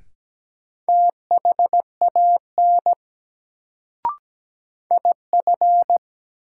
Morse code, heard as a keyed tone, about 17 wpm, 700 Hz. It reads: THAN IF
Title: Sentences from Top 100 Words - New Word Drill - 17wpm